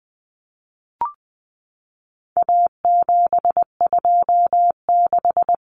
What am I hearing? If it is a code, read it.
Morse code, heard as a keyed tone, about 20 wpm, 700 Hz. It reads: A726
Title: Common Cut Numbers - 4-Digit 20wpm